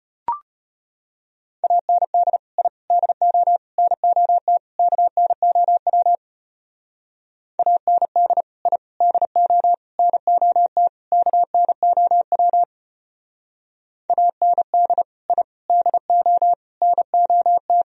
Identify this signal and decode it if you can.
Morse code, heard as a keyed tone, about 32 wpm, 700 Hz. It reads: AND I DO NOT KNOW AND I DO NOT KNOW AND I DO NOT